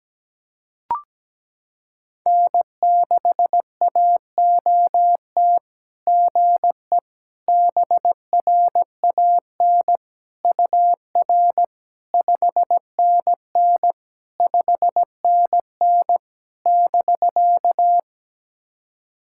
Morse code, heard as a keyed tone, about 17 wpm, 700 Hz. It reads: N6AOT GE BRAN UR 5NN 5NN <BK>
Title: SOTA QSO Practice 17wpm